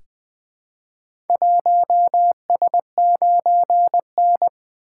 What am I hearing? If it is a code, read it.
Morse code, heard as a keyed tone, about 20 wpm, 700 Hz. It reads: 1S9N
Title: Four Letter-Number Combinations - 20wpm